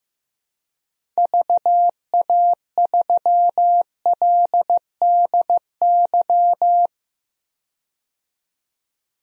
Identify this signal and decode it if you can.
Morse code, heard as a keyed tone, about 15 wpm, 700 Hz. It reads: VA3LDY
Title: DXCC-100 Call Signs - 6 Characters - 15wpm